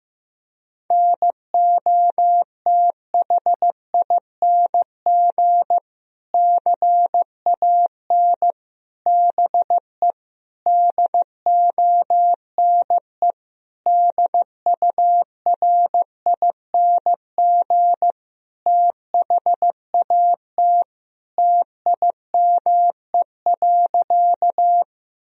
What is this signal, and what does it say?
Morse code, heard as a keyed tone, about 15 wpm, 700 Hz. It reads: NOTHING CAN BE DONE DURING THAT TIME.
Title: Sentences from Top 1000 Words - Review Part 2 - 15wpm